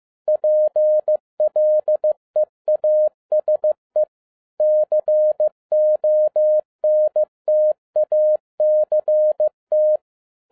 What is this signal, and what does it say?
Morse code, heard as a keyed tone, about 15 wpm, 610 Hz. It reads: PLEASE CONTACT
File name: 2011-10-mp3 / n12-JL3YMV 15wpm